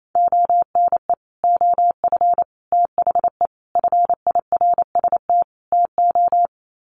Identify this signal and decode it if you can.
Morse code, heard as a keyed tone, about 28 wpm, 700 Hz. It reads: ONE OF THE FIRST TO
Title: Sets of 5 Words Encore 28wpm